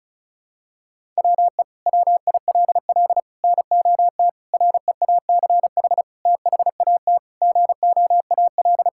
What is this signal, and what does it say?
Morse code, heard as a keyed tone, about 36 wpm, 700 Hz. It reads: WE WILL NOT REACH THAT GOAL
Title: Sentences from Top 1000 Words - Review Part 5 - 35wpm